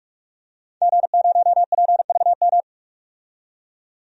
Morse code, heard as a keyed tone, about 44 wpm, 700 Hz. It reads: G0PVM
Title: DXCC-100 Call Signs - 5 Characters - 45wpm